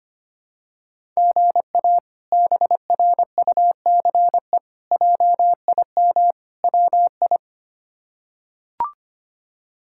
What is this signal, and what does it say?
Morse code, heard as a keyed tone, about 25 wpm, 700 Hz. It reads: GA BRUCE JIM WI
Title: K1USN SST QSO Practice - 25wpm